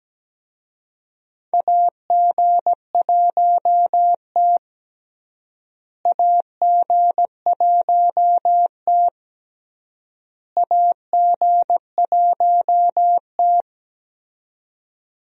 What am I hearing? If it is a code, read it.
Morse code, heard as a keyed tone, about 17 wpm, 700 Hz. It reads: AG1T AG1T AG1T
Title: USA Call Signs - 4 Characters - Repeated 3x - 17wpm